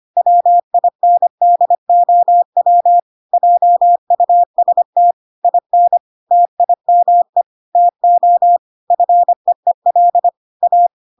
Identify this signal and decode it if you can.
Morse code, heard as a keyed tone, about 25 wpm, 700 Hz. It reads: WINDOW JUST IN TIME TO FEEL A